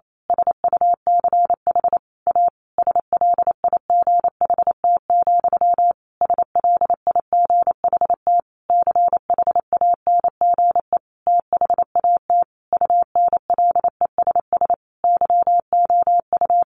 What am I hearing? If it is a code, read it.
Morse code, heard as a keyed tone, about 28 wpm, 700 Hz. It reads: SUCH A SLIGHT, SLIGHT CHANGE THAT UNLESS YOU
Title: A Sound Of Thunder 28wpm